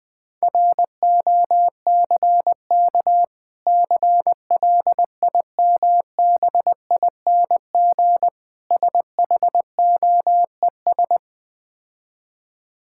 Morse code, heard as a keyed tone, about 20 wpm, 700 Hz. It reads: ROCK CLIMBING SHOES
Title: Difficult Words with Hints - 20wpm